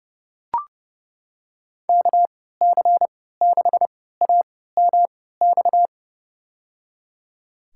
Morse code, heard as a keyed tone, about 30 wpm, spaced as for 17 wpm, 700 Hz. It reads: KC6AMX
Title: USA Call Signs - 6 Characters - 3x Character Spacing - 30wpm